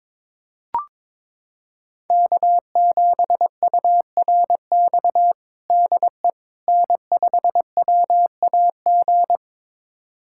Morse code, heard as a keyed tone, about 22 wpm, 700 Hz. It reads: K7URX DE N5WAG
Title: USA Callsign DE Callsign - 5 Characters - 22wpm